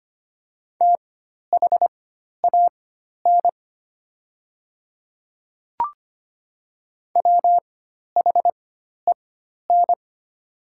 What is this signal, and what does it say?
Morse code, heard as a keyed tone, about 25 wpm, spaced as for 12 wpm, 690 Hz. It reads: THAN WHEN
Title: 4x Character Spacing - Top 100 Words - 25wpm